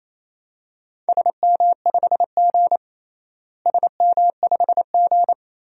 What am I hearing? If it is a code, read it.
Morse code, heard as a keyed tone, about 28 wpm, 700 Hz. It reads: SM5G SM5G
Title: DXCC-100 Call Signs - 4 Characters - Repeated 2x - 28wpm